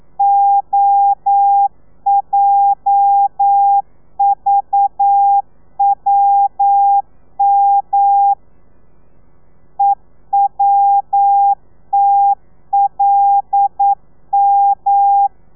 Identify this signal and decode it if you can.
Morse code, heard as a keyed tone, about 9 wpm, 800 Hz. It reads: OJVWM EWTLM
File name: telegrafia / i2tfj / Lez8 40